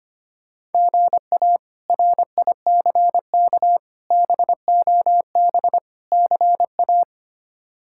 Morse code, heard as a keyed tone, about 25 wpm, 700 Hz. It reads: GA RICK BOB CA
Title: K1USN SST QSO Practice - 25wpm